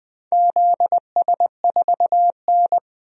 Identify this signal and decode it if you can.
Morse code, heard as a keyed tone, about 20 wpm, 700 Hz. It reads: ZS4N
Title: DXCC-100 Call Signs - 4 Characters - Repeated 3x - 20wpm